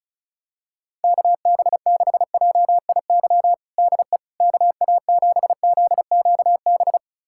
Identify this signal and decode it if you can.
Morse code, heard as a keyed tone, about 36 wpm, 700 Hz. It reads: KB6JIY DE KA7ZQB
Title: USA Callsign DE Callsign - 6 Characters - 35wpm